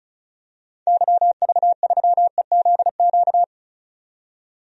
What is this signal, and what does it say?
Morse code, heard as a keyed tone, about 36 wpm, 700 Hz. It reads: YV3EZQ
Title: DXCC-100 Call Signs - 6 Characters - 35wpm